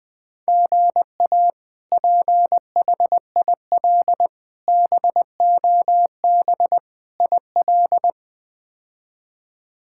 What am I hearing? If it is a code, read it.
Morse code, heard as a keyed tone, about 20 wpm, 700 Hz. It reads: GA PHIL BOB IL
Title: K1USN SST QSO Practice - 20wpm